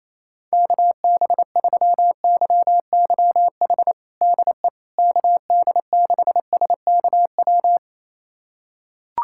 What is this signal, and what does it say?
Morse code, heard as a keyed tone, about 28 wpm, 700 Hz. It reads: KB3YYH DE KD6SKW
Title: USA Callsign DE Callsign - 6 Characters - 28wpm